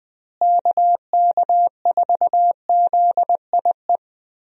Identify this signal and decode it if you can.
Morse code, heard as a keyed tone, about 21 wpm, 700 Hz. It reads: KK4ZIE
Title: USA Call Signs - 6 Characters - Repeated 3x - 20wpm